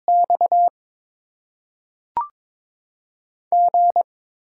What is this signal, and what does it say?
Morse code, heard as a keyed tone, about 22 wpm, 700 Hz. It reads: X G
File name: Top 20 QSO Elements - 8x Character Spacing - 22wpm